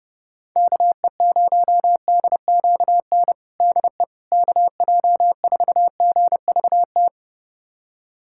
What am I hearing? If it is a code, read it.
Morse code, heard as a keyed tone, about 31 wpm, 700 Hz. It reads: KE0DQN DE KJ4GVT
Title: USA Callsign DE Callsign - 6 Characters - 30wpm